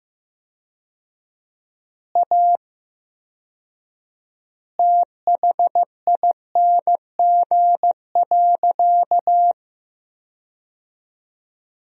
Morse code, heard as a keyed tone, about 15 wpm, 700 Hz. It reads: A THING.
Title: 4x Word-Spacing - Sentences from Top 200 Words - 15wpm